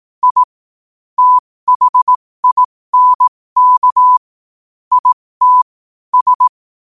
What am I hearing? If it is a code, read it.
Morse code, heard as a keyed tone, about 18 wpm, spaced as for 15 wpm, 1000 Hz. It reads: I THINK IT S
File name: qotdfiles / 15wpm